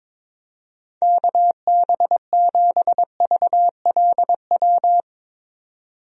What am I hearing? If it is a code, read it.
Morse code, heard as a keyed tone, about 22 wpm, 700 Hz. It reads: KB7VLW
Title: USA Call Signs - 6 Characters - 22wpm